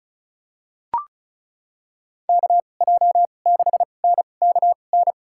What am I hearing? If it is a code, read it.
Morse code, heard as a keyed tone, about 36 wpm, spaced as for 25 wpm, 700 Hz. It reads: KJ6NKN
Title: USA Call Signs - 2x Character Spacing - 35wpm